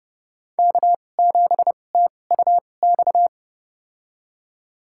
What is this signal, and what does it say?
Morse code, heard as a keyed tone, about 29 wpm, spaced as for 22 wpm, 700 Hz. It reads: K7TUX
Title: USA Call Signs - 5 Characters - 2x Character Spacing - 30wpm